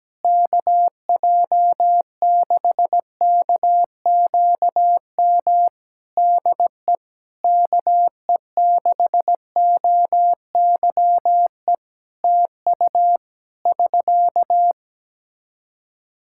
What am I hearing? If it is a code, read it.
Morse code, heard as a keyed tone, about 17 wpm, 700 Hz. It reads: KJ6KQM DE KE6OYE TU <SK>
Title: Practice QSO - Exchange 4 - Ending - 17wpm